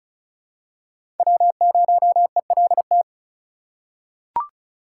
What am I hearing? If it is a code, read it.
Morse code, heard as a keyed tone, about 36 wpm, 700 Hz. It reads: W0ELT
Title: USA Call Signs - 5 Characters - 4x Character Spacing - 35wpm